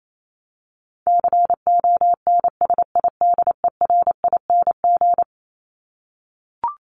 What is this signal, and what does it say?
Morse code, heard as a keyed tone, about 28 wpm, 700 Hz. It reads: CONSIDERING
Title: ING Suffix - Top 100 Words with Base Words 28wpm